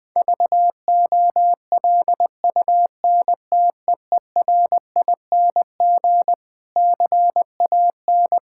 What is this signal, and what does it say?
Morse code, heard as a keyed tone, about 20 wpm, 700 Hz. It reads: VOLUNTEERING CAN
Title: Difficult Sentences with Hints - 20wpm